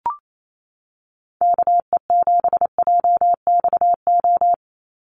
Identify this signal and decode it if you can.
Morse code, heard as a keyed tone, about 28 wpm, 700 Hz. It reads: KE7JXO
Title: USA Call Signs 28wpm